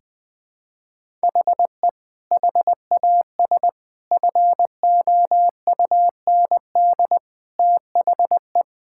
Morse code, heard as a keyed tone, about 20 wpm, 700 Hz. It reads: HE HAS FOUND THE
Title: Sentences from Top 1000 Words - Review Part 5 - 20wpm